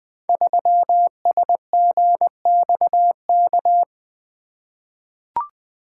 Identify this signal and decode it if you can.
Morse code, heard as a keyed tone, about 20 wpm, 700 Hz. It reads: 3SGXK
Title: Code Groups - 5 Letter:Numbers - 20wpm